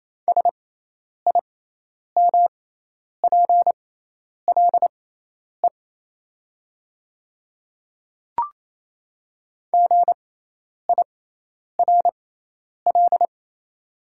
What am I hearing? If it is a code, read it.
Morse code, heard as a keyed tone, about 28 wpm, spaced as for 10 wpm, 700 Hz. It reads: SIMPLE GIRL
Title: 6x Character Spacing - Top 500 Words - 28wpm